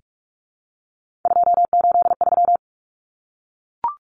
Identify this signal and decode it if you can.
Morse code, heard as a keyed tone, about 44 wpm, 700 Hz. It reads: 283
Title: Common Cut Numbers - 3-Digit 45wpm